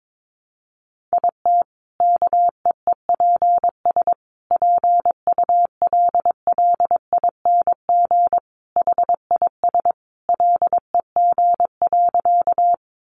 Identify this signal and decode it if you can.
Morse code, heard as a keyed tone, about 22 wpm, 700 Hz. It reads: IT KEEPS PULLING HIS LEG.
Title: Additional Easy Phrases 22wpm